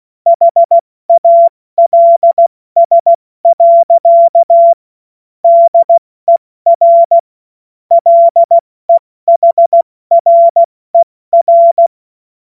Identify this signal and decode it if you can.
Morse code, heard as a keyed tone, about 16 wpm, 680 Hz. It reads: HALS. DER LEHRER
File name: Morsekurs / Witze / Witz 13. September 2022 70 F80